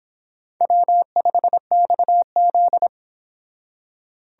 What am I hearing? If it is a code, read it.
Morse code, heard as a keyed tone, about 25 wpm, 700 Hz. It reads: W5XZ